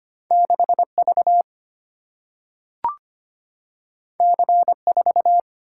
Morse code, heard as a keyed tone, about 25 wpm, 700 Hz. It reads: 6 V C 4
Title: Single-Letter Single-Number 25wpm